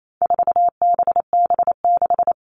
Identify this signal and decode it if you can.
Morse code, heard as a keyed tone, about 28 wpm, 700 Hz. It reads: 4BB6
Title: All Cut Numbers - 4-Digit 28wpm